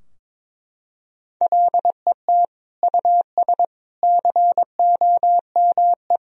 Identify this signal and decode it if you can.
Morse code, heard as a keyed tone, about 22 wpm, 700 Hz. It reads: LET US COME
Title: Sentences from Top 1000 Words - Review Part 3 - 22wpm